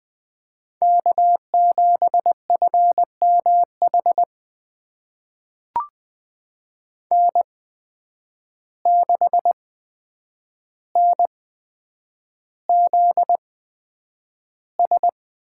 Morse code, heard as a keyed tone, about 20 wpm, spaced as for 9 wpm, 700 Hz. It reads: K7FMH N6NZS
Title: USA Call Signs - 5 Characters - 8x Character Spacing - 20wpm